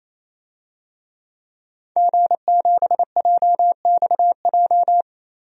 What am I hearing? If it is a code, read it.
Morse code, heard as a keyed tone, about 28 wpm, 700 Hz. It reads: G7JXJ
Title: DXCC-100 Call Signs - 5 Characters - 28wpm